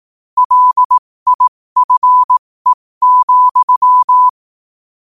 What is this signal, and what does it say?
Morse code, heard as a keyed tone, about 18 wpm, spaced as for 15 wpm, 1000 Hz. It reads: LIFE,
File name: qotdfiles / 15wpm